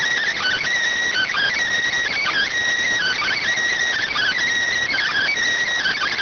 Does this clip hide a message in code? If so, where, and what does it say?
Yes: Emergency Alert System (SAME) tones (header bursts) from 1.1 to 5.3 s, heard as harsh buzzing bursts.